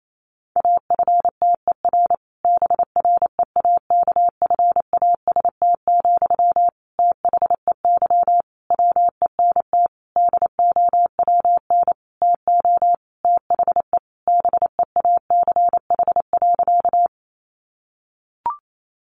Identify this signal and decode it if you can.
Morse code, heard as a keyed tone, about 28 wpm, 700 Hz. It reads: AFTER BREAKFAST, THEY WENT DOWN TO THE BEACH.
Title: Sentences from Top 1500 Words 28wpm